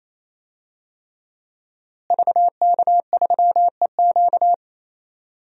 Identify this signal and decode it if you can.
Morse code, heard as a keyed tone, about 27 wpm, 700 Hz. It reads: VK3EQ
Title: DXCC-100 Call Signs - 5 Characters - Repeated 3x - 28wpm